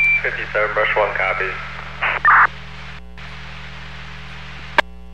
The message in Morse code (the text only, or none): none